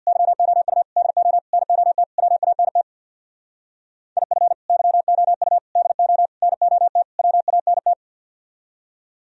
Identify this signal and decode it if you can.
Morse code, heard as a keyed tone, about 64 wpm, 700 Hz. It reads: YOU DO NOT WANT IF YOU DO NOT WANT